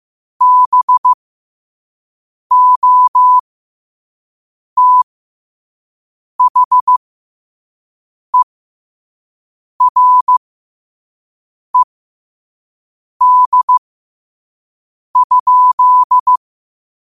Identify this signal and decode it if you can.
Morse code, heard as a keyed tone, about 15 wpm, spaced as for 5 wpm, 1000 Hz. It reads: BOTHERED?